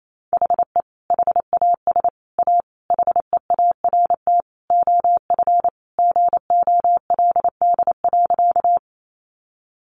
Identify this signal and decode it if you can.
Morse code, heard as a keyed tone, about 28 wpm, 700 Hz. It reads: HE HAS A HEART OF GOLD.
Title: Sentences from Top 1200 Words 28wpm